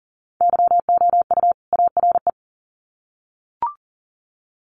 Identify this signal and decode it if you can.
Morse code, heard as a keyed tone, about 40 wpm, 700 Hz. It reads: YOU ARE
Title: Sets of 2 Words 40wpm